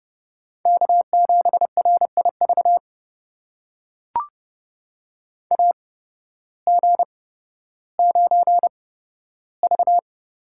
Morse code, heard as a keyed tone, about 30 wpm, spaced as for 14 wpm, 700 Hz. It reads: K7RIV AG9V
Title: USA Call Signs - 8x Character Spacing - 30wpm